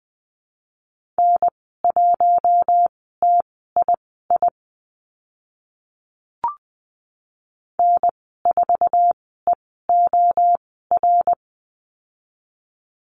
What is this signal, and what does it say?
Morse code, heard as a keyed tone, about 20 wpm, spaced as for 14 wpm, 700 Hz. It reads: N1TII N4EOR
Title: 062 Lesson 11g - Letter h - Call Signs 20wpm